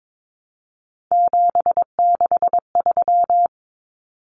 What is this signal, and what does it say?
Morse code, heard as a keyed tone, about 22 wpm, 700 Hz. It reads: 763
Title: Numbers - 3-Digit 22wpm